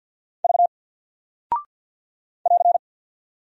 Morse code, heard as a keyed tone, about 48 wpm, 700 Hz. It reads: V <AR>
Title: Single Character (Letter:Number:Punctuation:Prosign) Alt 50wpm